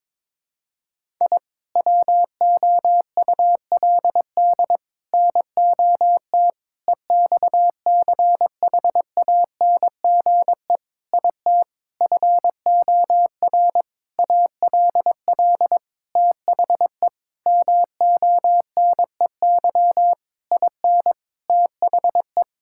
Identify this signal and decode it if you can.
Morse code, heard as a keyed tone, about 22 wpm, 700 Hz. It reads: I WOULD NOT EXCHANGE IT FOR ALL THE MONEY IN THE
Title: Sentences from Top 1000 Words - Review Part 6 - 22wpm